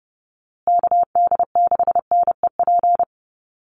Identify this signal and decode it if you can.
Morse code, heard as a keyed tone, about 30 wpm, 700 Hz. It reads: KD6NEP
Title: USA Call Signs Repeated 3x 30wpm